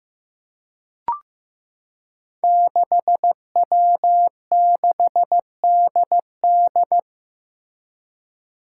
Morse code, heard as a keyed tone, about 15 wpm, 700 Hz. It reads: 6W6DD
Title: Five Challenging Letter-Number Combinations 15wpm